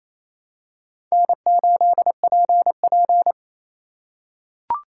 Morse code, heard as a keyed tone, about 28 wpm, 700 Hz. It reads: N8PP
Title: USA Call Signs - 4 Characters - 2x Character Spacing - 28wpm